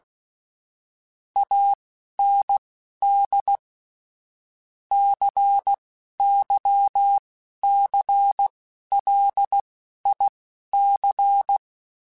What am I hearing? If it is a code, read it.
Morse code, heard as a keyed tone, about 16 wpm, spaced as for 11 wpm, 800 Hz. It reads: AND CYCLIC
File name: Technical Subjects / Technical Subjects-mp3s / XC @10